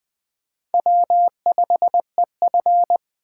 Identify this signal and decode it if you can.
Morse code, heard as a keyed tone, about 20 wpm, 700 Hz. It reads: W5EF
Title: USA Call Signs - 4 Characters - 6x Character Spacing - 20wpm